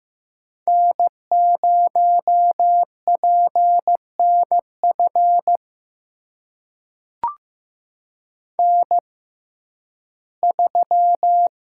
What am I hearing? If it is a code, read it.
Morse code, heard as a keyed tone, about 15 wpm, spaced as for 11 wpm, 700 Hz. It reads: N0PNF N3
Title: USA Call Signs - 5 Characters - 6x Character Spacing - 15wpm